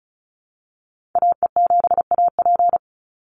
Morse code, heard as a keyed tone, about 34 wpm, 700 Hz. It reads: AE7AP
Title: NA SOTA Chaser Callsigns 35wpm